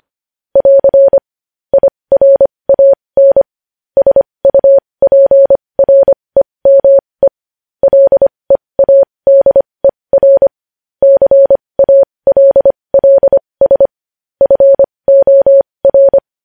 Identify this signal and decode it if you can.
Morse code, heard as a keyed tone, about 25 wpm, spaced as for 20 wpm, 550 Hz. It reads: <AR> IRAN SUPREME LEADER CALLS FOR
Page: news-headlines-in-morse-code-at-20-wpm